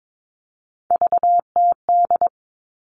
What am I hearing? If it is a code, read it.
Morse code, heard as a keyed tone, about 22 wpm, 700 Hz. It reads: VTD
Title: All Cut Numbers - 3-Digit 22wpm